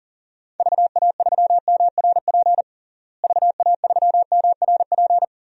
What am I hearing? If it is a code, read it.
Morse code, heard as a keyed tone, about 40 wpm, 700 Hz. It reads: VA3MRP VA3MRP
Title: DXCC-100 Call Signs - 6 Characters - Repeated 2x - 40wpm